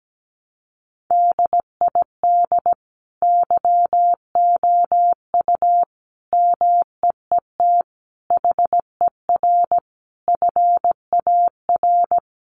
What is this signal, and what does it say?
Morse code, heard as a keyed tone, about 17 wpm, 700 Hz. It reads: DID YOU MEET HER FAR
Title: Sentences from Top 1100 Words 17wpm